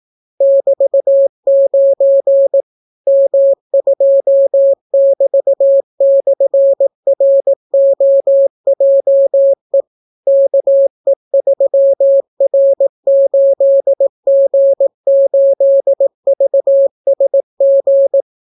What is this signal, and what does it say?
Morse code, heard as a keyed tone, about 18 wpm, 550 Hz. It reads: <BT>9 M2<BT>/ROJE KE3R8G8VSG